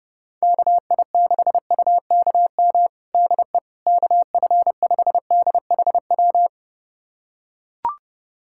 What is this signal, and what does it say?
Morse code, heard as a keyed tone, about 29 wpm, 700 Hz. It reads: KI6UKM DE KF5DHW
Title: USA Callsign DE Callsign - 6 Characters - 30wpm